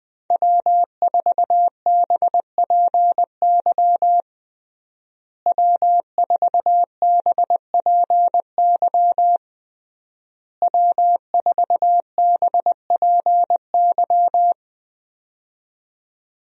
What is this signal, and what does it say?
Morse code, heard as a keyed tone, about 20 wpm, 700 Hz. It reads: W4BPY W4BPY W4BPY
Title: USA Call Signs - 5 Characters - Repeated 3x - 20wpm